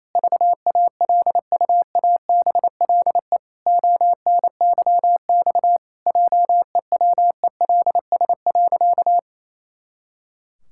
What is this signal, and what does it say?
Morse code, heard as a keyed tone, about 28 wpm, 700 Hz. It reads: VALUABLE ONYX JEWELS.
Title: Pangram Challenge v2 - 28wpm